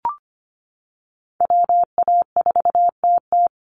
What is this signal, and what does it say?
Morse code, heard as a keyed tone, about 25 wpm, 700 Hz. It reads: WA4TT
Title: Call Signs Repeated 3x - 25wpm